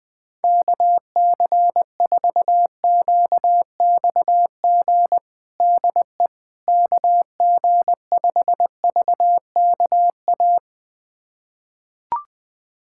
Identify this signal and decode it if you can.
Morse code, heard as a keyed tone, about 20 wpm, 700 Hz. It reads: KC4QXG DE KG5VKA
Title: USA Callsign DE Callsign - 6 Characters - 20wpm